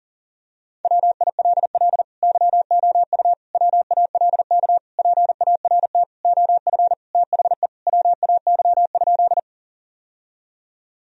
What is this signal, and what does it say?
Morse code, heard as a keyed tone, about 40 wpm, 700 Hz. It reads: WILL YOU WALK PART OF THE WAY?
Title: Sentences from Top 1000 Words - Review Part 1 - 40wpm